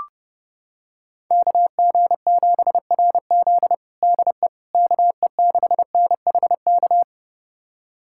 Morse code, heard as a keyed tone, about 30 wpm, 700 Hz. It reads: KG7RZ DE KE6NHK
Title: USA Callsign DE Callsign 30wpm